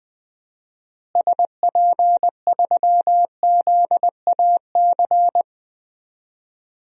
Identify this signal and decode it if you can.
Morse code, heard as a keyed tone, about 20 wpm, 700 Hz. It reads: SP3ZAC
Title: DXCC-100 Call Signs - 6 Characters - Repeated 3x - 20wpm